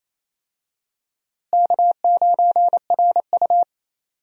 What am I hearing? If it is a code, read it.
Morse code, heard as a keyed tone, about 28 wpm, 700 Hz. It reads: K9RU
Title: USA Call Signs - 4 Characters - Repeated 2x - 28wpm